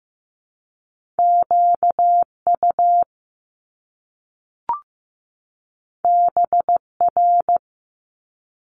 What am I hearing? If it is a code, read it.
Morse code, heard as a keyed tone, about 15 wpm, 700 Hz. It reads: QU BR
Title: Top Two Letter Combinations in Words 15wpm